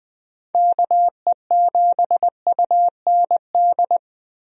Morse code, heard as a keyed tone, about 20 wpm, 700 Hz. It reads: KE7UND
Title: USA Call Signs - 6 Characters - Repeated 3x - 20wpm